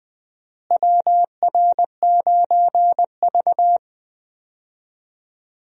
Morse code, heard as a keyed tone, about 20 wpm, 700 Hz. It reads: WR9V
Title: USA Call Signs - 4 Characters - Repeated 3x - 20wpm